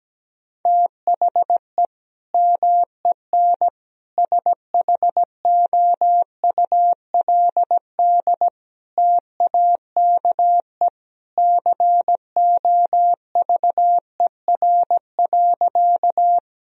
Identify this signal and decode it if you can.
Morse code, heard as a keyed tone, about 17 wpm, 700 Hz. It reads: THE MEN SHOULD TAKE COVER.
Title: Sentences from Top 1000 Words - Review Part 1 - 17wpm